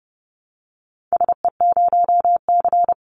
Morse code, heard as a keyed tone, about 30 wpm, 700 Hz. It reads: SE0C
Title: Call Signs Repeated 2x - 30wpm